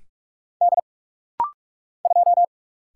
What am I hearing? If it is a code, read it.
Morse code, heard as a keyed tone, about 46 wpm, 700 Hz. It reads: D 2
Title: Single Letter-Number Alt - Rapid-Fire 45wpm